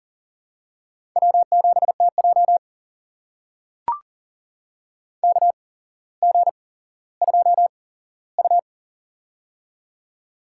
Morse code, heard as a keyed tone, about 40 wpm, spaced as for 17 wpm, 700 Hz. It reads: W7TJ KG2U
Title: USA Call Signs - 4 Characters - 8x Character Spacing - 40wpm